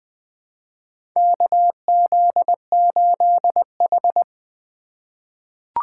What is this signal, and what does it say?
Morse code, heard as a keyed tone, about 20 wpm, 700 Hz. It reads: KZ8H
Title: USA Call Signs - 4x Character Spacing - 20wpm